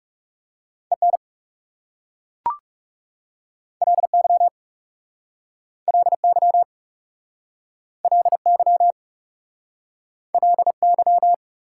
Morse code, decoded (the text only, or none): EN LY LY LY LY